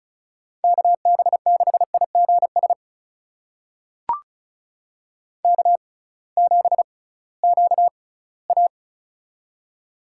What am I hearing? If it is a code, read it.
Morse code, heard as a keyed tone, about 36 wpm, spaced as for 21 wpm, 700 Hz. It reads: KB6IGS K7QA
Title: USA Call Signs - 6x Character Spacing - 35wpm